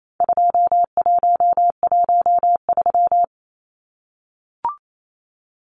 Morse code, heard as a keyed tone, about 28 wpm, 700 Hz. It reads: 2113
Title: Numbers - 4-Digit 28wpm